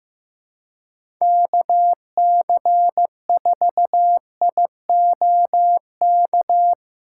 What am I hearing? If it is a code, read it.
Morse code, heard as a keyed tone, about 15 wpm, 700 Hz. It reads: KC4IOK
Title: USA Callsign DE Callsign - 6 Characters - 15wpm